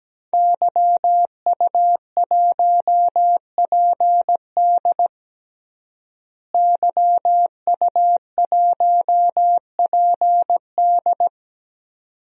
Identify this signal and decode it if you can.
Morse code, heard as a keyed tone, about 17 wpm, 700 Hz. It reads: YU1PD YU1PD
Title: DXCC-100 Call Signs - 5 Characters - Repeated 2x - 17wpm